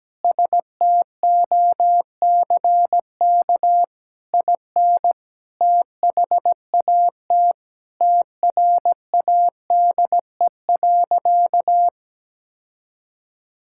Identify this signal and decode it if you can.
Morse code, heard as a keyed tone, about 17 wpm, 700 Hz. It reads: STOCK IN THAT TRADE.
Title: Sentences from Top 1000 Words - Review Part 5 - 17wpm